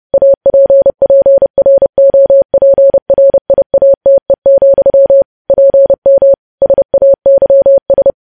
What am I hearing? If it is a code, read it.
Morse code, heard as a keyed tone, about 30 wpm, 550 Hz. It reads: APPROPRIATE, PM SAYS
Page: news headlines in morse code at 30 wpm